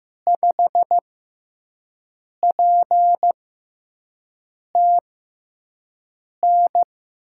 Morse code, heard as a keyed tone, about 15 wpm, spaced as for 5 wpm, 700 Hz. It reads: 5PTN
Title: USA Call Signs - 6x Character Spacing - 15wpm